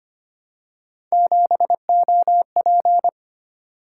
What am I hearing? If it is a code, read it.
Morse code, heard as a keyed tone, about 25 wpm, 700 Hz. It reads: 7OP
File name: Three Letter-Number Combinations - 25wpm